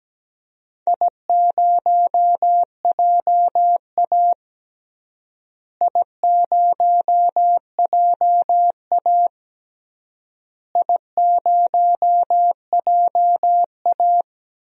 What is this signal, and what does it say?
Morse code, heard as a keyed tone, about 17 wpm, 700 Hz. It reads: I0JA I0JA I0JA
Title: DXCC-100 Call Signs - 4 Characters - Repeated 3x - 17wpm